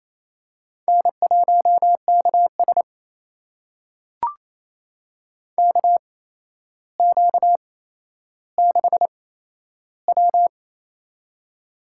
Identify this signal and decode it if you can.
Morse code, heard as a keyed tone, about 28 wpm, spaced as for 12 wpm, 700 Hz. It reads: N1KS KQ6W
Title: USA Call Signs - 4 Characters - 8x Character Spacing - 28wpm